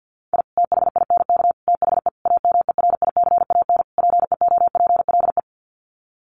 Morse code, heard as a keyed tone, about 50 wpm, 700 Hz. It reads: I THINK THE AMERICAN PEOPLE
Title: Sets of 5 Words Encore 50wpm